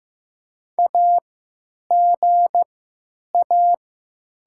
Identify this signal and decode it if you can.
Morse code, heard as a keyed tone, about 15 wpm, spaced as for 9 wpm, 700 Hz. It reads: AGA
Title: 3x Character Spacing - Top 300 Words - 15wpm